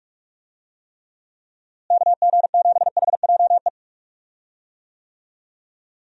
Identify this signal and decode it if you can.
Morse code, heard as a keyed tone, about 42 wpm, 700 Hz. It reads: KG7HJE
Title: USA Call Signs - 6 Characters - Repeated 3x - 45wpm